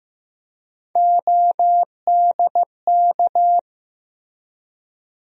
Morse code, heard as a keyed tone, about 16 wpm, 700 Hz. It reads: ODK
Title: USA Call Sign Postfixes - 3 Letters 15wpm